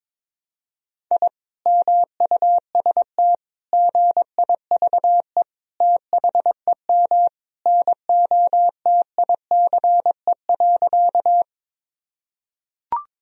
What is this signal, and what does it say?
Morse code, heard as a keyed tone, about 22 wpm, 700 Hz. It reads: I MUST GIVE THEM NOTICE.
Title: Sentences from Top 1000 Words - Review Part 2 - 22wpm